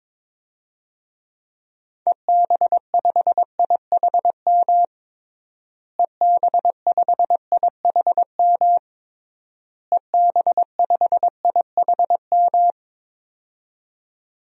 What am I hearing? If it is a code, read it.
Morse code, heard as a keyed tone, about 22 wpm, 700 Hz. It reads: EB5IHM EB5IHM EB5IHM
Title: DXCC-100 Call Signs - 6 Characters - Repeated 3x - 22wpm